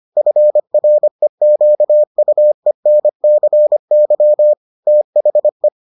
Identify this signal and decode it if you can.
Morse code, heard as a keyed tone, about 25 wpm, 600 Hz. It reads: FREQUENCY THE